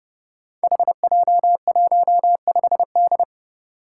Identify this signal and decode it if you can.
Morse code, heard as a keyed tone, about 30 wpm, 700 Hz. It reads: HJ15D
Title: Five Challenging Letter-Number Combinations 30wpm